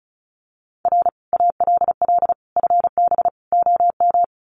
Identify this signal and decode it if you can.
Morse code, heard as a keyed tone, about 34 wpm, 700 Hz. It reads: R ALL FB OM
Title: Common QSO Phrases 35wpm